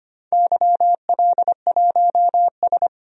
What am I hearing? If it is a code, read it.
Morse code, heard as a keyed tone, about 25 wpm, 700 Hz. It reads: YL1S
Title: DXCC-100 Call Signs - 4 Characters - 25wpm